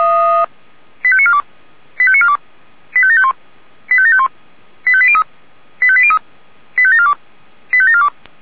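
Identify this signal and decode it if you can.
Five-tone selective call (ZVEI), heard as quick runs of five tones, tones of about 70 ms. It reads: 85942 85942 85751 85751 85803 85803 85742 85742
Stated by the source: ZVEI five-tone selective call 85725, received by SDR.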